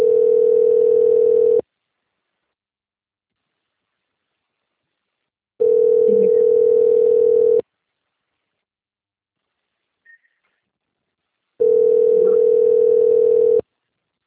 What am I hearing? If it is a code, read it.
Ringback tone (North America), 440 + 480 Hz.
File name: out- - - - - .82